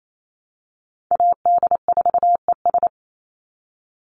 Morse code, heard as a keyed tone, about 28 wpm, 700 Hz. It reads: AD4ES
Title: Call Signs Repeated 3x - 28wpm